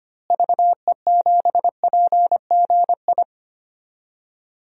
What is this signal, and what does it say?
Morse code, heard as a keyed tone, about 25 wpm, 700 Hz. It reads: VE7PGI
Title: DXCC-100 Call Signs - 6 Characters - Repeated 2x - 25wpm